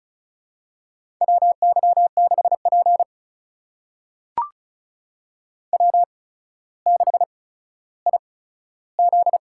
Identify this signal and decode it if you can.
Morse code, heard as a keyed tone, about 36 wpm, spaced as for 15 wpm, 700 Hz. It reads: WY6P W6IZ
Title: USA Call Signs - 4 Characters - 8x Character Spacing - 35wpm